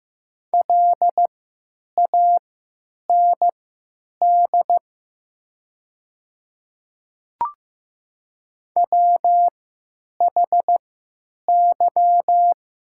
Morse code, heard as a keyed tone, about 15 wpm, spaced as for 9 wpm, 700 Hz. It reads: LAND WHY
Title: 3x Character Spacing - Top 200 Words - 15wpm